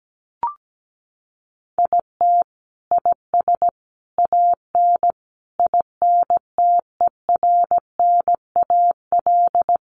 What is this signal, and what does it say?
Morse code, heard as a keyed tone, about 17 wpm, 700 Hz. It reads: IT IS AN INTERNAL
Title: Sentences from Top 1400 Words 17wpm